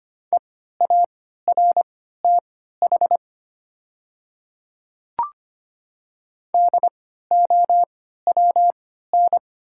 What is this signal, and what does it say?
Morse code, heard as a keyed tone, about 25 wpm, spaced as for 14 wpm, 700 Hz. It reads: EARTH DOWN
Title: 3x Character Spacing - Top 200 Words - 25wpm